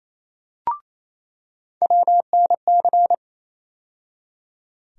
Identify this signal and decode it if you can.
Morse code, heard as a keyed tone, about 28 wpm, 700 Hz. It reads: WNC
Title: USA Call Sign Postfixes - 3 Letters 28wpm